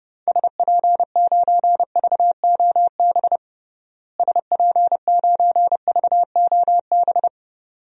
Morse code, heard as a keyed tone, about 30 wpm, 700 Hz. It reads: SP9VOB SP9VOB
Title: DXCC-100 Call Signs - 6 Characters - Repeated 2x - 30wpm